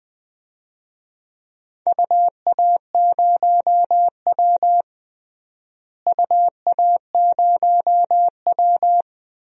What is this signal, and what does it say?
Morse code, heard as a keyed tone, about 20 wpm, 700 Hz. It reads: UA0W UA0W
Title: DXCC-100 Call Signs - 4 Characters - Repeated 3x - 20wpm